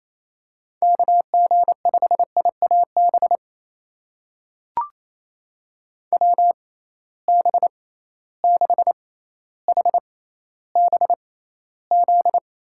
Morse code, heard as a keyed tone, about 28 wpm, spaced as for 14 wpm, 700 Hz. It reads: KG5IAB WB6HBZ
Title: USA Call Signs - 6 Characters - 6x Character Spacing - 28wpm